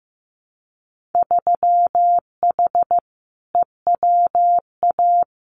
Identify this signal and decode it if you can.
Morse code, heard as a keyed tone, about 15 wpm, 700 Hz. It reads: 3H EWA
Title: Winter Field Day Exchanges Encore 15wpm